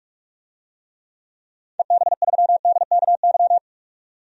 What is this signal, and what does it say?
Morse code, heard as a keyed tone, about 47 wpm, 700 Hz. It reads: EB3DKY
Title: DXCC-100 Call Signs - 6 Characters - 45wpm